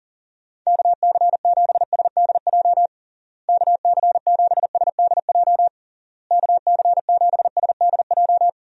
Morse code, heard as a keyed tone, about 40 wpm, 700 Hz. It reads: KC7SDJ KC7SDJ KC7SDJ
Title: USA Call Signs - 6 Characters - Repeated 3x - 40wpm